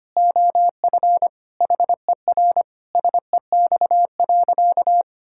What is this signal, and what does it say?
Morse code, heard as a keyed tone, about 25 wpm, 700 Hz. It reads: OF HER SEX.
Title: Sentences from Top 1600 Words 25wpm